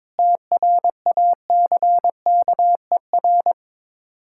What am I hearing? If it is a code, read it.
Morse code, heard as a keyed tone, about 22 wpm, 690 Hz. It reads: TRACKER
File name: Difficult Words with Hints - 22wpm